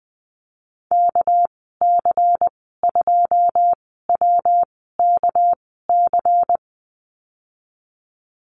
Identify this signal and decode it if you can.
Morse code, heard as a keyed tone, about 20 wpm, spaced as for 14 wpm, 700 Hz. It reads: KC2WKC
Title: 231 Lesson 37g - Letter x - Call Signs 20wpm